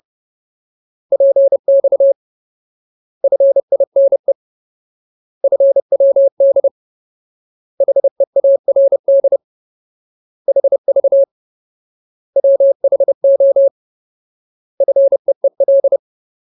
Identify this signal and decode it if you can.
Morse code, heard as a keyed tone, about 30 wpm, 550 Hz. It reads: PX FINE FWD HEARD HV WHO FEEL
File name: Ultimate Word List / 30WPM Ultimate